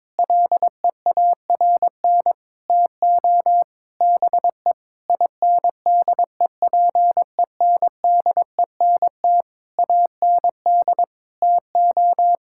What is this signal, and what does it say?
Morse code, heard as a keyed tone, about 22 wpm, 700 Hz. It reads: LEARN TO BE INDEPENDENT AND TO
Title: Sentences from Top 1000 Words - Review Part 6 - 22wpm